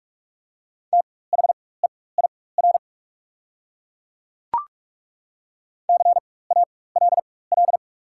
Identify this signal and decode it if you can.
Morse code, heard as a keyed tone, about 42 wpm, spaced as for 20 wpm, 700 Hz. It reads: THEIR CALL
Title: 4x Character Spacing - Top 300 Words - 45wpm